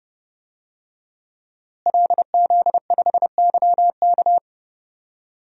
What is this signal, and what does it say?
Morse code, heard as a keyed tone, about 30 wpm, 700 Hz. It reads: LZ5YK
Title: DXCC-100 Call Signs - 5 Characters - Repeated 2x - 30wpm